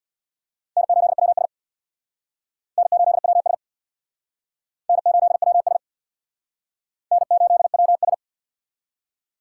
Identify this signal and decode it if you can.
Morse code, heard as a keyed tone, about 65 wpm, 700 Hz. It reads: N8WS N8WS N8WS N8WS